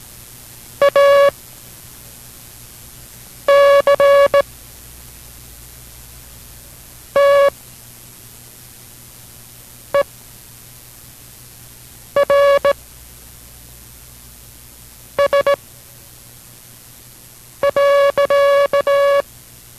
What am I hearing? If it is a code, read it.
Morse code, heard as a keyed tone, about 17 wpm, spaced as for 4 wpm, 560 Hz. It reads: ACTERS.